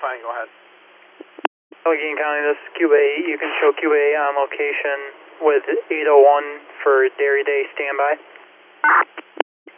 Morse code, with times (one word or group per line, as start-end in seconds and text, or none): none